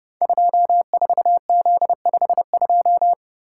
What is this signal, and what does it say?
Morse code, heard as a keyed tone, about 30 wpm, 700 Hz. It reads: 24Z52
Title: Code Groups - 5 Letter:Numbers - 30wpm